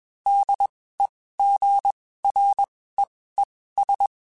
Morse code, heard as a keyed tone, about 21 wpm, spaced as for 15 wpm, 800 Hz. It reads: DEGREES
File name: (spaced) / QSOs Spaced-mp3s / WD0FBT @20 sp